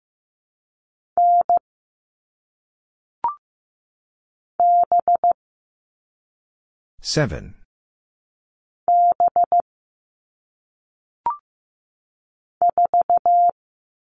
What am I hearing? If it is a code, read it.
Morse code, heard as a keyed tone, about 15 wpm, 700 Hz. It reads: N B B 4
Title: All Cut Numbers - 1-Digit 15wpm